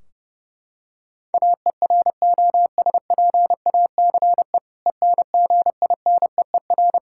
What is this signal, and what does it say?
Morse code, heard as a keyed tone, about 29 wpm, 700 Hz. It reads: AEROSPACE ENGINEER
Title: Practice QSO - Exchange 3 - More Info - 30wpm